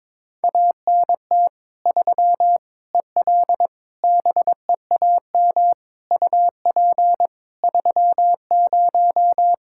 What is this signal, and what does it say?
Morse code, heard as a keyed tone, about 22 wpm, 700 Hz. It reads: ANT 3 EL BEAM UP 30
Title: Practice QSO - Exchange 2 - Info - 22wpm